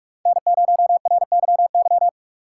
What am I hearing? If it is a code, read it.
Morse code, heard as a keyed tone, about 43 wpm, 700 Hz. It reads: N0RYY
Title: USA Call Signs - 5 Characters - 3x Character Spacing - 45wpm